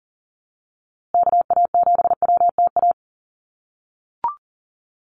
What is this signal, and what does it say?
Morse code, heard as a keyed tone, about 40 wpm, 700 Hz. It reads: KA7WTA
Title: USA Call Signs Repeated 3x 40wpm